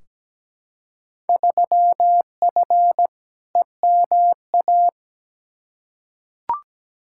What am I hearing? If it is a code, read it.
Morse code, heard as a keyed tone, about 17 wpm, 700 Hz. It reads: 3F EMA
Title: Field Day Exchanges Encore - 17wpm